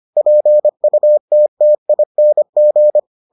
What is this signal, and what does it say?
Morse code, heard as a keyed tone, about 25 wpm, 600 Hz. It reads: PUTTING